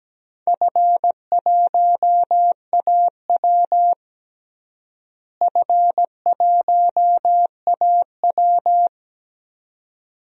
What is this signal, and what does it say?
Morse code, heard as a keyed tone, about 17 wpm, 700 Hz. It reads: F1AW F1AW
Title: DXCC-100 Call Signs - 4 Characters - Repeated 2x - 17wpm